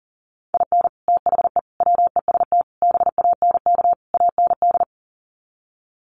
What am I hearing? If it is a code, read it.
Morse code, heard as a keyed tone, about 40 wpm, 700 Hz. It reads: IN THE WEST BANK AND
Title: Sets of 5 Words Encore2 40wpm